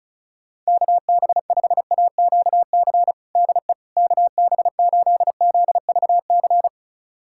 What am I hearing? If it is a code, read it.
Morse code, heard as a keyed tone, about 36 wpm, 700 Hz. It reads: KB5AQC DE KB8ZVC
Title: USA Callsign DE Callsign - 6 Characters - 35wpm